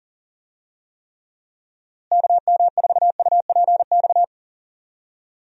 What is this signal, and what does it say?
Morse code, heard as a keyed tone, about 40 wpm, 700 Hz. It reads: KM4UPX
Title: USA Call Signs - 6 Characters - Repeated 2x - 40wpm